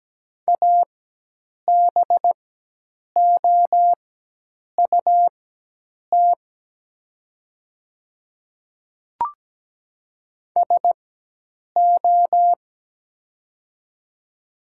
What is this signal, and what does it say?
Morse code, heard as a keyed tone, about 17 wpm, spaced as for 8 wpm, 700 Hz. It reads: ABOUT SO
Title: 4x Character Spacing - Top 100 Words - 17wpm